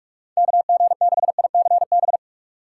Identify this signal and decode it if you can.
Morse code, heard as a keyed tone, about 46 wpm, 700 Hz. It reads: KG6ICB
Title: USA Call Signs - 6 Characters - 3x Character Spacing - 45wpm